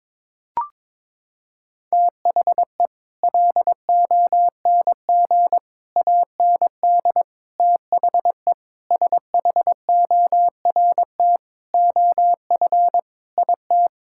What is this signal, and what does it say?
Morse code, heard as a keyed tone, about 22 wpm, 700 Hz. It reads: THE LONG AND THE SHORT OF IT
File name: Sets of 9 Words - 22wpm